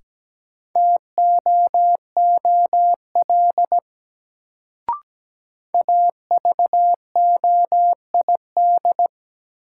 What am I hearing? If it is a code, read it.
Morse code, heard as a keyed tone, about 17 wpm, 700 Hz. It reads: TOOL AVOID
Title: Sentences from Top 1000 Words - Review Part 6 - New Word Drill - 17wpm